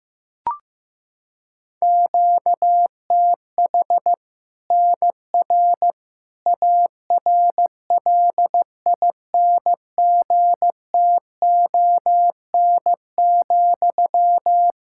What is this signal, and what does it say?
Morse code, heard as a keyed tone, about 15 wpm, 700 Hz. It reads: QTH NR ARLINGTON,
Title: K1USN SST QSO Practice 15wpm